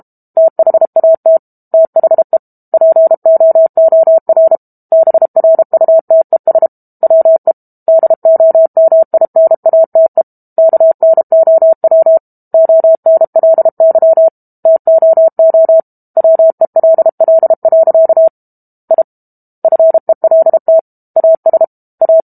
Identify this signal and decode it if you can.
Morse code, heard as a keyed tone, about 32 wpm, 650 Hz. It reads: THAT THE POOR BRUTES WE DOMINATE KNOW ONLY TOO WELL. I FELT AS A
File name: Book Two / 06-Chapter 6 - 32 WPM